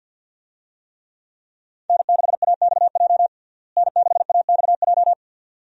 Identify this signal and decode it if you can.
Morse code, heard as a keyed tone, about 51 wpm, 700 Hz. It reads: N6AXJ N6AXJ
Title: USA Call Signs - 5 Characters - Repeated 2x - 50wpm